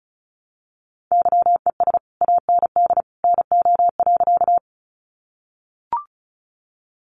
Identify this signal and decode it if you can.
Morse code, heard as a keyed tone, about 34 wpm, 700 Hz. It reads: YES AND NO.
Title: Sentences from Top 500 Words 35wpm